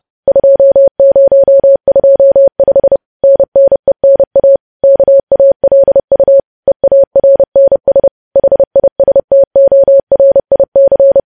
Morse code, heard as a keyed tone, about 30 wpm, 550 Hz. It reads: 2025 NNENA KALU EARNS HISTORIC